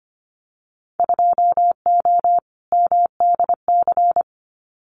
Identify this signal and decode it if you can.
Morse code, heard as a keyed tone, about 25 wpm, 700 Hz. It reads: 2O MDC
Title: Winter Field Day Exchanges 25wpm